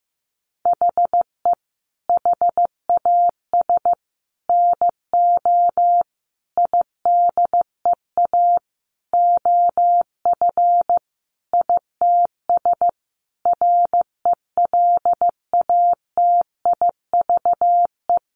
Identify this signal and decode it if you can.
Morse code, heard as a keyed tone, about 15 wpm, 700 Hz. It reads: HE HAS NO IDEA OF ITS RELATIVE
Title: Sentences from Top 1400 Words 15wpm